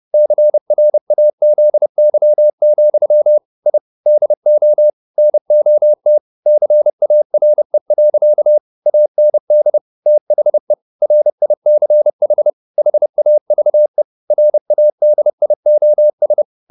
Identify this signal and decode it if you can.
Morse code, heard as a keyed tone, about 30 wpm, 600 Hz. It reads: CRAZY, I DO NOT CARE. AND THE RICH HAVE RADIOS
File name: the old man and the sea / the old man and the sea 1 30wpm